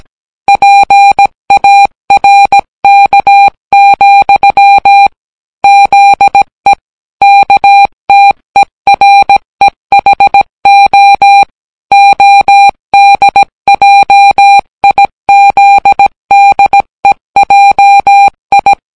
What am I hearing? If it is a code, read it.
Morse code, heard as a keyed tone, about 17 wpm, 800 Hz. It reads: PARK, ZE KTEREHO ODJIZDEJI